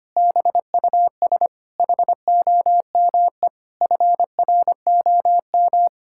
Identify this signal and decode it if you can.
Morse code, heard as a keyed tone, about 25 wpm, 700 Hz. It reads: BUS HOME FROM
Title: Sentences from Top 1000 Words - Review Part 6 - 25wpm